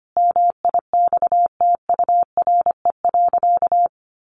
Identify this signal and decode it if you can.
Morse code, heard as a keyed tone, about 25 wpm, 700 Hz. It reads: MIXTURE.
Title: Sentences from Top 1500 Words 25wpm